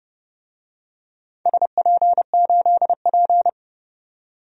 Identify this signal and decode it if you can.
Morse code, heard as a keyed tone, about 30 wpm, 700 Hz. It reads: SP8P
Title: DXCC-100 Call Signs - 4 Characters - 30wpm